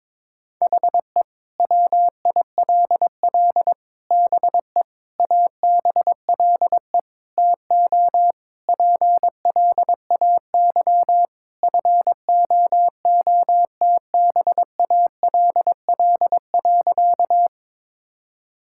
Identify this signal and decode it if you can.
Morse code, heard as a keyed tone, about 22 wpm, 700 Hz. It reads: HE WILL BE ABLE TO PLAY FOOTBALL.
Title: Sentences from Top 1600 Words 22wpm